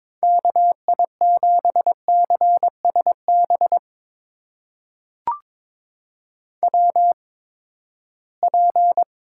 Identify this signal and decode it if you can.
Morse code, heard as a keyed tone, about 22 wpm, spaced as for 15 wpm, 700 Hz. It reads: KI7CSB WP
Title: USA Call Signs - 6 Characters - 8x Character Spacing - 22wpm